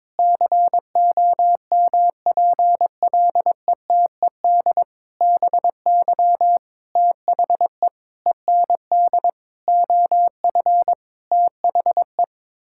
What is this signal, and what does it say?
Morse code, heard as a keyed tone, about 22 wpm, 700 Hz. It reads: COMPLETED BY THE END OF THE
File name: practice / 22wpm / Sentences with 8-Word Phrases - 22wpm